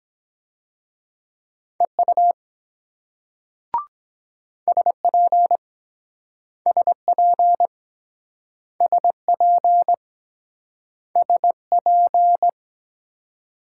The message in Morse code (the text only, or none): EU SP SP SP SP